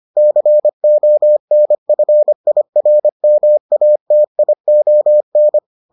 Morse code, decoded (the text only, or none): CONFIRMATION